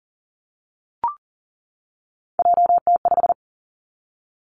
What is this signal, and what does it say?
Morse code, heard as a keyed tone, about 42 wpm, 700 Hz. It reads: JT5
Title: Three Challenging Letter-Number Combinations 40wpm